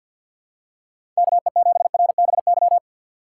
Morse code, heard as a keyed tone, about 49 wpm, 700 Hz. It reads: KE7RBY
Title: USA Call Signs - 6 Characters - 4x Character Spacing - 50wpm